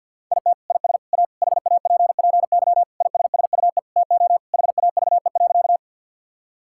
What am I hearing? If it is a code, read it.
Morse code, heard as a keyed tone, about 50 wpm, 700 Hz. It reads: IT IS A HAPPY ISSUE TO HAVE.
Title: Sentences from Top 1000 Words - Review Part 6 - 50wpm